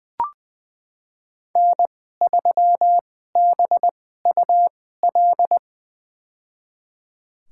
Morse code, heard as a keyed tone, about 20 wpm, spaced as for 15 wpm, 700 Hz. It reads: N3BUL
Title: USA Call Signs - 2x Character Spacing - 20wpm